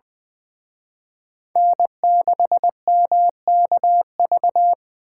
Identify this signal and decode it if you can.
Morse code, heard as a keyed tone, about 23 wpm, 700 Hz. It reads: N6MKV
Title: Speed Racing - USA 1x3 Call Signs 20wpm